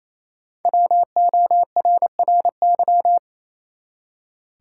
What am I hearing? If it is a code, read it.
Morse code, heard as a keyed tone, about 28 wpm, 700 Hz. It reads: WORRY
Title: Five Letter Words - 28wpm